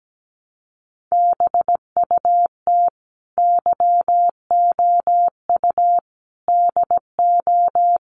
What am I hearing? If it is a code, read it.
Morse code, heard as a keyed tone, about 17 wpm, 700 Hz. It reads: BUT YOU DO
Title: Sets of 5 Words Encore 17wpm